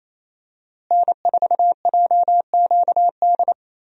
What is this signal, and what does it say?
Morse code, heard as a keyed tone, about 28 wpm, 700 Hz. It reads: N4JQD
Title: USA Call Signs - 5 Characters - Repeated 3x - 28wpm